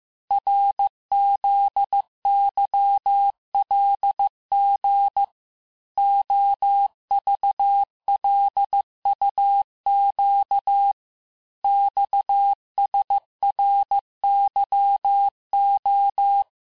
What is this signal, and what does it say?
Morse code, heard as a keyed tone, about 15 wpm, 800 Hz. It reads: RZYLG OVLUQ XSRYO
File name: cours-du-lundi files / cours-1585-texte-chiffre-vitesse-14mpm